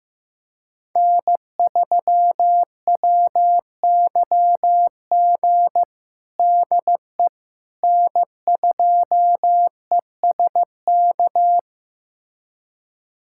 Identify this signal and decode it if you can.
Morse code, heard as a keyed tone, about 15 wpm, 700 Hz. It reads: N3WYG DE N2ESK
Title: USA Callsign DE Callsign - 5 Characters - 15wpm